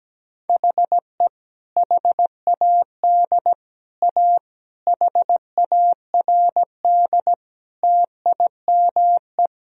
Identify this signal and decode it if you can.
Morse code, heard as a keyed tone, about 17 wpm, 700 Hz. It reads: HE HAD A HARD TIME
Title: Sentences from Top 1000 Words - Review Part 1 - 17wpm